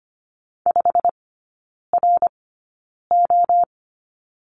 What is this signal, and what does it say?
Morse code, heard as a keyed tone, about 25 wpm, spaced as for 9 wpm, 700 Hz. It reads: 5RO
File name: course / 25:10wpm / 093 Lesson 16e - Letter c - Three Characters 25wpm